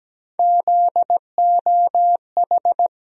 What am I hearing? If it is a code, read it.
Morse code, heard as a keyed tone, about 17 wpm, 700 Hz. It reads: ZOH
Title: USA Call Sign Postfixes - 3 Letters 17wpm